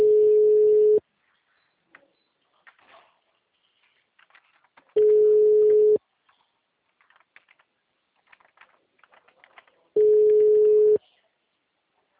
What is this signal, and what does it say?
Ringback tone (Europe, 425 Hz).